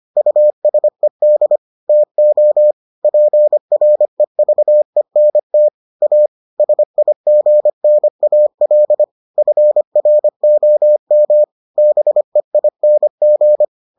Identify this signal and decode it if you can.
Morse code, heard as a keyed tone, about 25 wpm, 600 Hz. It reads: USED TO PREVENT A SIGNAL FROM BEING